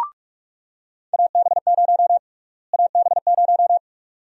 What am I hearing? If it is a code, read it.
Morse code, heard as a keyed tone, about 47 wpm, 700 Hz. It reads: AB0 AB0
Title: USA Call Sign Prefixes - Repeated 2x - 45wpm